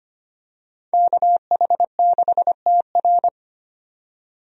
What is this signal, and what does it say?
Morse code, heard as a keyed tone, about 25 wpm, 700 Hz. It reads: KH6TR
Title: DXCC-100 Call Signs - 5 Characters - 25wpm